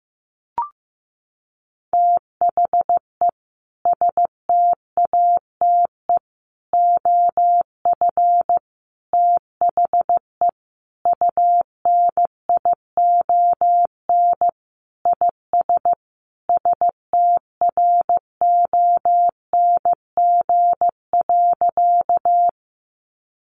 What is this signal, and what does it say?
Morse code, heard as a keyed tone, about 15 wpm, 700 Hz. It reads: THE STATE OF THE UNION IS STRONG.
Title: Sentences from Top 1400 Words 15wpm